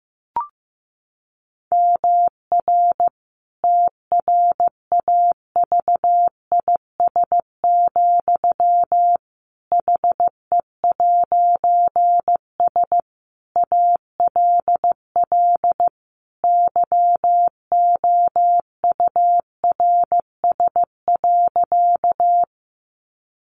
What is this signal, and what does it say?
Morse code, heard as a keyed tone, about 15 wpm, 700 Hz. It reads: MR TRAVIS, HE'S ALL YOURS.
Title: A Sound Of Thunder 15wpm